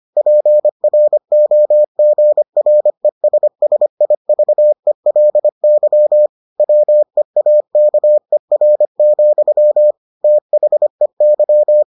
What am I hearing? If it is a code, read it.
Morse code, heard as a keyed tone, about 25 wpm, 600 Hz. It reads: PROGRESSIVELY WEAKER, THEY